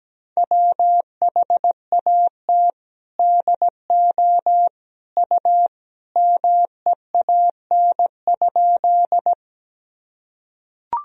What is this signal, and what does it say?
Morse code, heard as a keyed tone, about 17 wpm, 700 Hz. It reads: WHAT DO U MEAN?
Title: Sentences from Top 200 Words and Abbreviations 17wpm